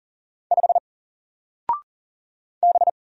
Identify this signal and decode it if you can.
Morse code, heard as a keyed tone, about 40 wpm, 700 Hz. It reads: 5 B
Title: Single Letter-Number Alt 40wpm